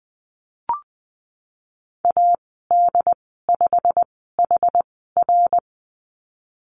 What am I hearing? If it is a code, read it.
Morse code, heard as a keyed tone, about 20 wpm, spaced as for 15 wpm, 700 Hz. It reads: AD5HR
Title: 088 Lesson 15g - Number 5 - Call Signs 20wpm